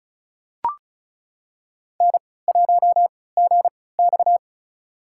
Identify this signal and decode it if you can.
Morse code, heard as a keyed tone, about 34 wpm, spaced as for 20 wpm, 700 Hz. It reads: N1GX
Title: USA Call Signs - 4 Characters - 3x Character Spacing - 35wpm